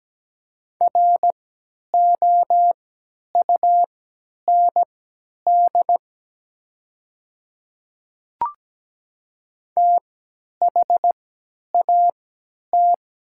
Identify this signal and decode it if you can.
Morse code, heard as a keyed tone, about 17 wpm, spaced as for 10 wpm, 700 Hz. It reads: ROUND THAT
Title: 3x Character Spacing - Top 200 Words - 17wpm